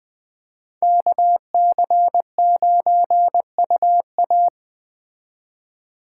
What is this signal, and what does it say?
Morse code, heard as a keyed tone, about 20 wpm, 700 Hz. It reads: KC9UA
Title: Speed Racing - USA 2x2 Call Signs 17wpm